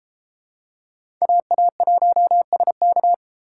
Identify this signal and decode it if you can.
Morse code, heard as a keyed tone, about 32 wpm, 700 Hz. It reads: AA1SK